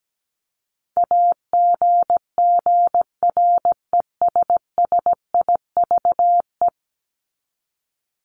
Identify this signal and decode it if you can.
Morse code, heard as a keyed tone, about 17 wpm, 700 Hz. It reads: AGGRESSIVE
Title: Top 1501-1750 Words 17wpm